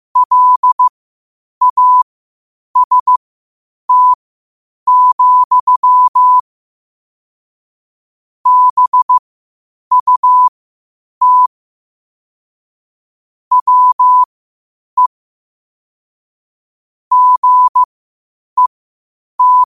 Morse code, heard as a keyed tone, about 15 wpm, spaced as for 8 wpm, 1000 Hz. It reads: LAST, BUT WE GET
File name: qotdfiles / 8wpm